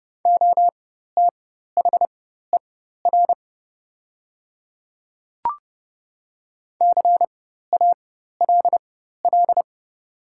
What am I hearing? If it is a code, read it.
Morse code, heard as a keyed tone, about 30 wpm, spaced as for 14 wpm, 700 Hz. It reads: OTHER CALL
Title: 4x Character Spacing - Top 200 Words - 30wpm